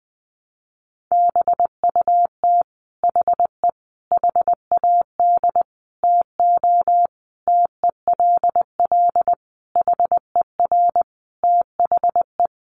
Morse code, heard as a keyed tone, about 20 wpm, 700 Hz. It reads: BUT HE HAD TO TELL HER THE
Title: Sentences from Top 800 Words 20wpm